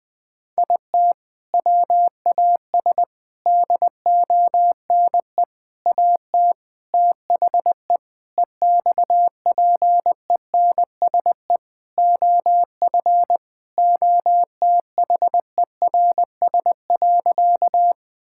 Morse code, heard as a keyed tone, about 20 wpm, 700 Hz. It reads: IT WAS DONE AT THE EXPENSE OF OTHERS.
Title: Sentences with 4-Word Phrases - 20wpm